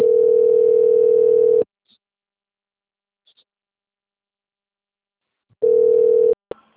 Ringback tone (North America), 440 + 480 Hz.